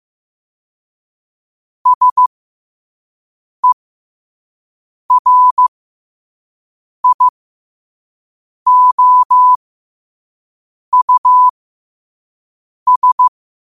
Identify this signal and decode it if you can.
Morse code, heard as a keyed tone, about 15 wpm, spaced as for 5 wpm, 1000 Hz. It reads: SERIOUS
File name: 2026-02-15-5wpm